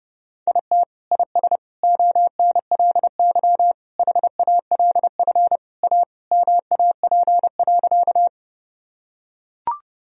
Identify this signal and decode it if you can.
Morse code, heard as a keyed tone, about 30 wpm, 700 Hz. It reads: IT IS ONLY HALF A MAP.
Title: Sentences from Top 1000 Words - Review Part 2 - 30wpm